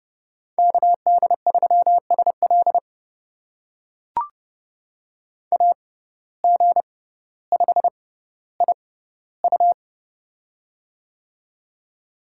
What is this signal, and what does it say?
Morse code, heard as a keyed tone, about 30 wpm, spaced as for 15 wpm, 700 Hz. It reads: KD3SL AG5IU
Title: USA Call Signs - 5 Characters - 6x Character Spacing - 30wpm